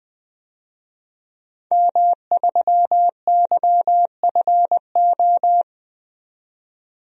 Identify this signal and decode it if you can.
Morse code, heard as a keyed tone, about 20 wpm, 700 Hz. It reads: M3YFO
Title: DXCC-100 Call Signs - 5 Characters - 20wpm